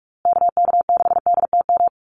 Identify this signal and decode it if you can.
Morse code, heard as a keyed tone, about 44 wpm, 700 Hz. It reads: KK6DTM
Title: USA Call Signs Repeated 3x 45wpm